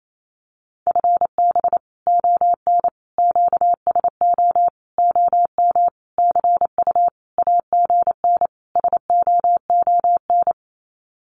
Morse code, heard as a keyed tone, about 28 wpm, 700 Hz. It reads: FB ON QSO OM CU AGN SOON
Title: Common QSO Phrases 28wpm